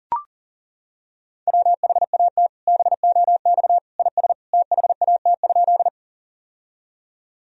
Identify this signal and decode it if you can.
Morse code, heard as a keyed tone, about 40 wpm, 700 Hz. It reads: WHAT BOX IS THAT?
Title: Sentences from Top 1000 Words - Review Part 2 - 40wpm